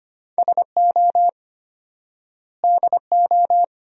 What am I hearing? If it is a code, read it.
Morse code, heard as a keyed tone, about 25 wpm, 700 Hz. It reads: SO DO
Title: 4x Word-Spacing - Sentences from Top 400 Words - 25wpm